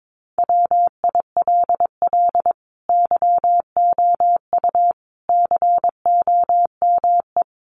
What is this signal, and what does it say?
Morse code, heard as a keyed tone, about 22 wpm, 700 Hz. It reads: WILL YOU COME
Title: Sentences from Top 1500 Words 22wpm